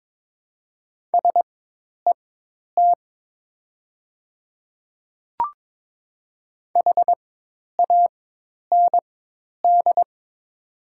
Morse code, heard as a keyed tone, about 22 wpm, spaced as for 10 wpm, 700 Hz. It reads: SET HAND
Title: 4x Character Spacing - Top 300 Words - 22wpm